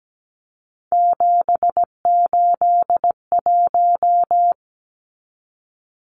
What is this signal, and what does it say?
Morse code, heard as a keyed tone, about 17 wpm, 700 Hz. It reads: 781
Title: Numbers - 3-Digit 17wpm